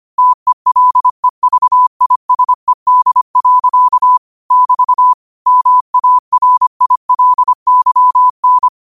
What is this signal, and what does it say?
Morse code, heard as a keyed tone, about 25 wpm, 1000 Hz. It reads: TELEVISED. <BT> MARILYN
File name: 25wpm